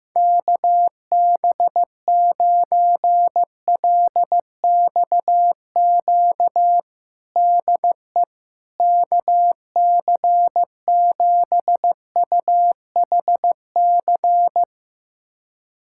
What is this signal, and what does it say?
Morse code, heard as a keyed tone, about 15 wpm, 700 Hz. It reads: KB9LXQ DE KC7UHC
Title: USA Callsign DE Callsign - 6 Characters - 15wpm